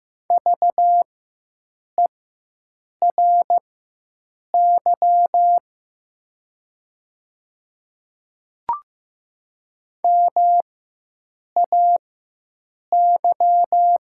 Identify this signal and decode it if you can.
Morse code, heard as a keyed tone, about 15 wpm, spaced as for 7 wpm, 700 Hz. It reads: VERY MAY
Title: 4x Character Spacing - Top 200 Words - 15wpm